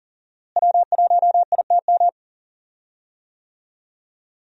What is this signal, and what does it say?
Morse code, heard as a keyed tone, about 40 wpm, 700 Hz. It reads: W1ITM
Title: USA Call Signs - 5 Characters - Repeated 2x - 40wpm